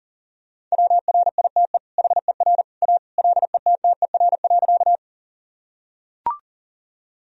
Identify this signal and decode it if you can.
Morse code, heard as a keyed tone, about 40 wpm, 700 Hz. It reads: WRITE HER A LETTER.
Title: Sentences from Top 1000 Words - Review Part 1 - 40wpm